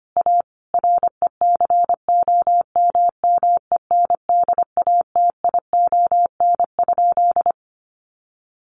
Morse code, heard as a keyed tone, about 25 wpm, 700 Hz. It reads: A RECOMMENDATION?
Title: Sentences from Top 1400 Words 25wpm